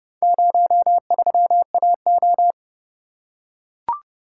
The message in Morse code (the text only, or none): 03AO